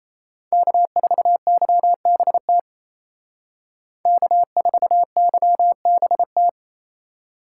Morse code, heard as a keyed tone, about 36 wpm, 700 Hz. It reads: K4YBT K4YBT